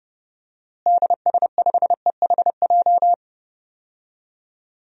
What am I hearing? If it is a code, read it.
Morse code, heard as a keyed tone, about 31 wpm, 700 Hz. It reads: DS5EHJ
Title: DXCC-100 Call Signs - 6 Characters - Repeated 2x - 30wpm